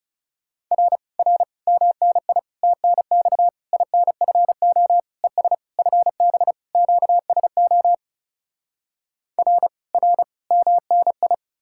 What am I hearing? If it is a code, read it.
Morse code, heard as a keyed tone, about 40 wpm, 700 Hz. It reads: R R MNI TNX INFO ES FB QSO R R MNI